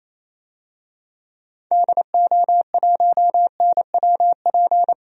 Morse code, heard as a keyed tone, about 28 wpm, 700 Hz. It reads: DO1NWP
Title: DXCC-100 Call Signs - 6 Characters - 28wpm